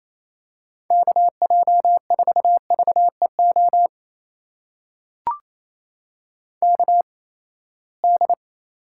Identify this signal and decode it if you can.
Morse code, heard as a keyed tone, about 28 wpm, spaced as for 19 wpm, 700 Hz. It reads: KJ4VEO KD
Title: USA Call Signs - 8x Character Spacing - 28wpm